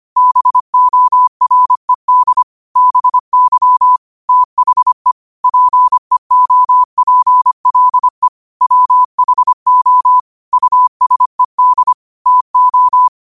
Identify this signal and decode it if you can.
Morse code, heard as a keyed tone, about 25 wpm, 1000 Hz. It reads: BORED BY THE PEOPLE WHO USED TO
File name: qotdfiles / 25wpm